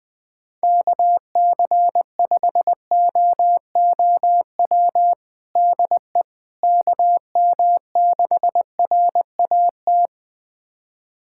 Morse code, heard as a keyed tone, about 20 wpm, 700 Hz. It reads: KC5OOW DE KM6RAT
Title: USA Callsign DE Callsign - 6 Characters - 20wpm